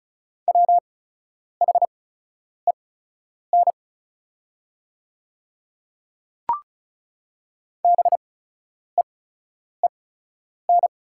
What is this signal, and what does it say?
Morse code, heard as a keyed tone, about 36 wpm, spaced as for 10 wpm, 690 Hz. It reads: WHEN BEEN
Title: 8x Character Spacing - Top 200 Words - 35wpm